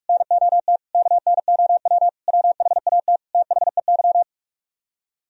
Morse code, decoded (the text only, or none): NOT KNOW WHAT THEY